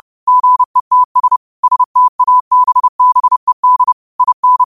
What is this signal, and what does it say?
Morse code, heard as a keyed tone, about 30 wpm, 1000 Hz. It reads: GETS STABBED IN
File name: qotdfiles / 30wpm